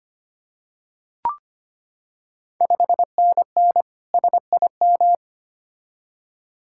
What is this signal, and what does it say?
Morse code, heard as a keyed tone, about 25 wpm, 700 Hz. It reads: 5NN SIM
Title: Ontario QSO Party Exchange 25wpm